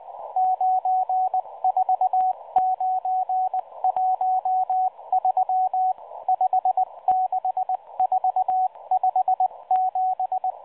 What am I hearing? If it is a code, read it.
Morse code, heard as a keyed tone, about 18 wpm, 750 Hz, with radio noise. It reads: 94913 56457
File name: HM02 5-14-15 0140 AM UTC 7351kHz CW